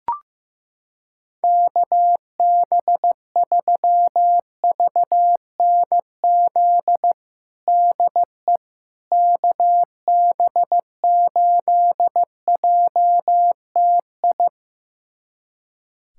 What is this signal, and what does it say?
Morse code, heard as a keyed tone, about 15 wpm, 700 Hz. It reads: KB3VNZ DE KB8JTI
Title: USA Callsign DE Callsign 15wpm